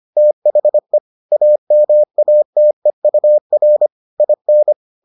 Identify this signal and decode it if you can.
Morse code, heard as a keyed tone, about 25 wpm, 600 Hz. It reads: THE AMATEUR IN